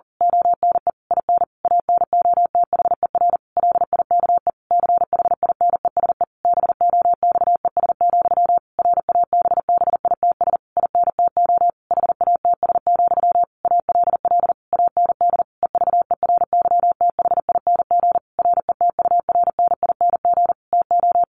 Morse code, heard as a keyed tone, about 40 wpm, 700 Hz. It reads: ONE IN ANOTHER LIKE CHINESE BOXES, RABBITS INTO HATS, ALL AND EVERYTHING RETURNING TO
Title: A Sound Of Thunder 40wpm